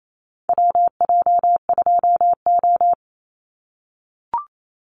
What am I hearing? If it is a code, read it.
Morse code, heard as a keyed tone, about 28 wpm, 700 Hz. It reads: WJ2O
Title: Call Signs - 28wpm